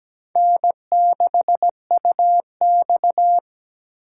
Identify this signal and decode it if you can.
Morse code, heard as a keyed tone, about 17 wpm, 700 Hz. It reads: N6UX
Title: USA Call Signs - 4 Characters - 2x Character Spacing - 17wpm